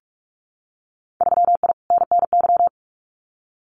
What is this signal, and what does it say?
Morse code, heard as a keyed tone, about 46 wpm, 700 Hz. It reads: 3I NNY
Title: Winter Field Day Exchanges 45wpm